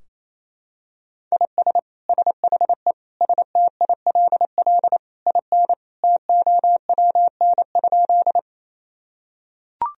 Morse code, heard as a keyed tone, about 28 wpm, 700 Hz. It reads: IS SHE STILL IN TOWN?
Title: Sentences from Top 1000 Words - Review Part 2 - 28wpm